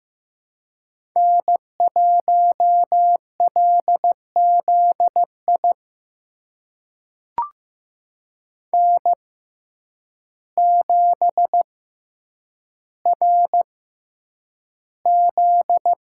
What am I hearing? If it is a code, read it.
Morse code, heard as a keyed tone, about 15 wpm, spaced as for 8 wpm, 700 Hz. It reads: N1LZI N7RZ
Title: USA Call Signs - 5 Characters - 6x Character Spacing - 15wpm